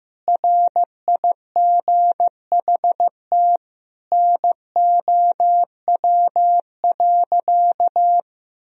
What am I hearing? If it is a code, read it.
Morse code, heard as a keyed tone, about 15 wpm, 700 Hz. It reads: RIGHT NOW.
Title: Sentences from Top 1000 Words - Review Part 4 - 15wpm